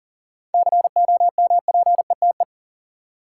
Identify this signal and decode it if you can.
Morse code, heard as a keyed tone, about 40 wpm, 700 Hz. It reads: COMPETE
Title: Seven Letter Words - 40wpm